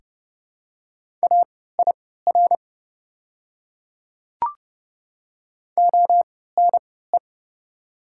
Morse code, heard as a keyed tone, about 32 wpm, spaced as for 17 wpm, 690 Hz. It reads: AIR ONE
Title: 3x Character Spacing - Top 200 Words - 30wpm